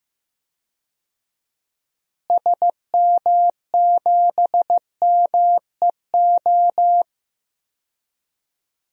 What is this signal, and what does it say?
Morse code, heard as a keyed tone, about 15 wpm, 700 Hz. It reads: SM7MEO
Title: DXCC-100 Call Signs - 6 Characters - Repeated 3x - 15wpm